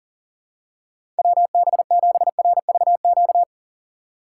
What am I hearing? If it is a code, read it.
Morse code, heard as a keyed tone, about 40 wpm, 700 Hz. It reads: WB7RVQ
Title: USA Call Signs - 6 Characters - 6x Character Spacing - 40wpm